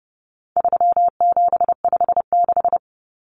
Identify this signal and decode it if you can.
Morse code, heard as a keyed tone, about 30 wpm, 700 Hz. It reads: 3756
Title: Common Cut Numbers - 4-Digit 30wpm